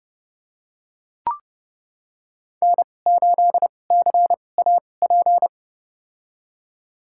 Morse code, heard as a keyed tone, about 30 wpm, spaced as for 22 wpm, 700 Hz. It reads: N8CAP
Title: USA Call Signs - 5 Characters - 2x Character Spacing - 30wpm